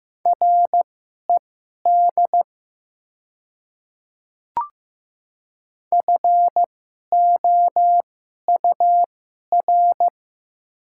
Morse code, heard as a keyed tone, about 15 wpm, spaced as for 11 wpm, 700 Hz. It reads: RED FOUR
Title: 2x Character Spacing - Top 500 Words - 15wpm